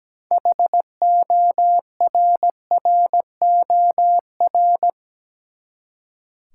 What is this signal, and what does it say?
Morse code, heard as a keyed tone, about 17 wpm, 700 Hz. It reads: HORROR
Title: Six Letter Words - 17wpm